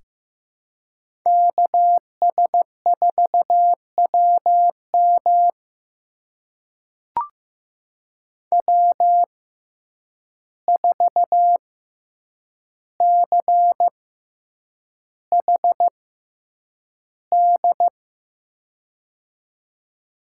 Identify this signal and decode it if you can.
Morse code, heard as a keyed tone, about 15 wpm, spaced as for 8 wpm, 700 Hz. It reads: KS4WM W4CHD
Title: USA Call Signs - 5 Characters - 6x Character Spacing - 15wpm